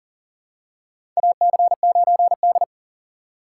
Morse code, heard as a keyed tone, about 40 wpm, 700 Hz. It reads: AC9D
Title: USA Call Signs - 4 Characters - 6x Character Spacing - 40wpm